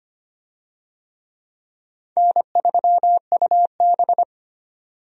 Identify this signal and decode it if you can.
Morse code, heard as a keyed tone, about 25 wpm, 700 Hz. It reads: N3UB
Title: USA Call Signs - 4 Characters - 25wpm